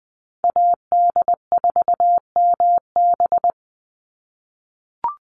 Morse code, heard as a keyed tone, about 20 wpm, 700 Hz. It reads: AD4MB
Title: USA Call Signs 20wpm